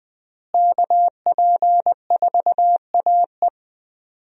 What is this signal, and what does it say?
Morse code, heard as a keyed tone, about 20 wpm, 700 Hz. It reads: KP4AE
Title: USA Call Signs - 5 Characters - Repeated 2x - 20wpm